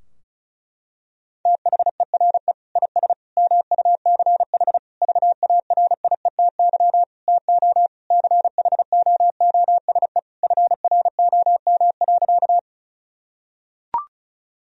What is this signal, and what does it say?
Morse code, heard as a keyed tone, about 36 wpm, 700 Hz. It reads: THERE IS MUCH VARIETY TO CHOOSE FROM.
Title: Sentences from Top 1000 Words - Review Part 4 - 35wpm